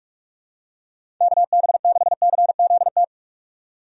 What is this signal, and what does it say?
Morse code, heard as a keyed tone, about 48 wpm, 700 Hz. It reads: KB6CZT
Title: USA Call Signs - 6 Characters - 45wpm